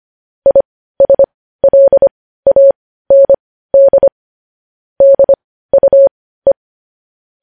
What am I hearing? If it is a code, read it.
Morse code, heard as a keyed tone, about 25 wpm, spaced as for 15 wpm, 550 Hz. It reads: ISLAND DUE
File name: podcast / News headlines-1x-25x15